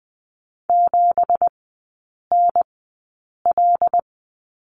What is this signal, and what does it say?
Morse code, heard as a keyed tone, about 20 wpm, spaced as for 8 wpm, 700 Hz. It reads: 7NL
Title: 166 Lesson 27e - Number 7 - Three Characters 20wpm